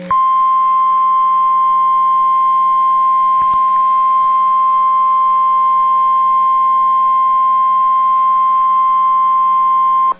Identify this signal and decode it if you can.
Emergency Alert System (SAME) tones (the 1050 Hz attention tone). It is heard from 0.8 to 9.5 s.